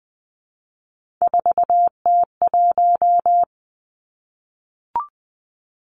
Morse code, heard as a keyed tone, about 19 wpm, 700 Hz. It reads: 4T1
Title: Three Challenging Letter-Number Combinations 20wpm